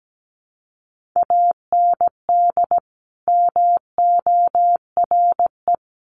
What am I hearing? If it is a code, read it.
Morse code, heard as a keyed tone, about 17 wpm, 700 Hz. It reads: AND MORE
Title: Sets of 2 Words 17wpm